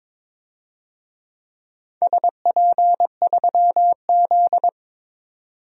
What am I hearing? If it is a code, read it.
Morse code, heard as a keyed tone, about 22 wpm, 700 Hz. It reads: SP3Z
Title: DXCC-100 Call Signs - 4 Characters - 22wpm